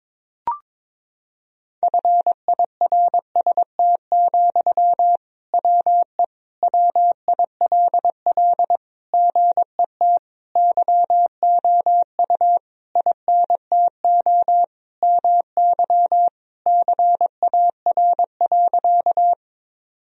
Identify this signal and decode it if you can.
Morse code, heard as a keyed tone, about 22 wpm, 700 Hz. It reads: FIRST, WE WILL GET YOU INTO MY CAR.
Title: Sentences from Top 1000 Words - Review Part 1 - 22wpm